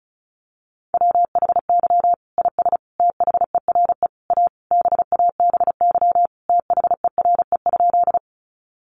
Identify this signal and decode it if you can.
Morse code, heard as a keyed tone, about 34 wpm, 700 Hz. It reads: WHY IS THERE A BABY THERE?
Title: Sentences from Top 1300 Words 35wpm